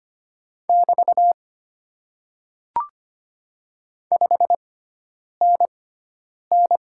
Morse code, heard as a keyed tone, about 25 wpm, spaced as for 9 wpm, 700 Hz. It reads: <BT> 5NN
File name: Top 90 QSO Elements - 6x Character Spacing - 25wpm